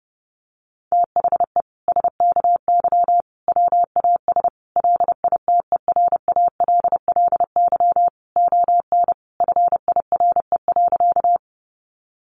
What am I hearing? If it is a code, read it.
Morse code, heard as a keyed tone, about 30 wpm, 700 Hz. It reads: THE SKY WAS LITERALLY ON FIRE.
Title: Sentences from Top 1400 Words 30wpm